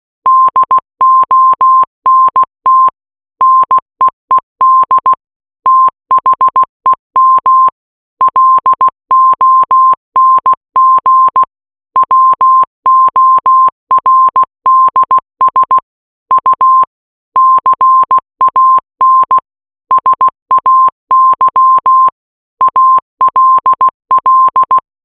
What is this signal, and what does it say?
Morse code, heard as a keyed tone, about 16 wpm, 1050 Hz. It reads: DONT NEED THEM LONG WORDS U CAN SAY ALL